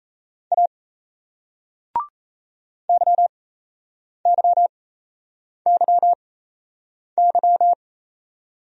Morse code, heard as a keyed tone, about 37 wpm, spaced as for 8 wpm, 700 Hz. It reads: A YYYY